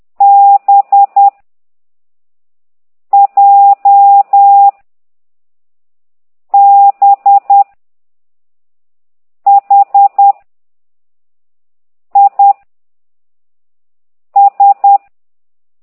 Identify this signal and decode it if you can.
Morse code, heard as a keyed tone, about 10 wpm, 800 Hz. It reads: B J B H I S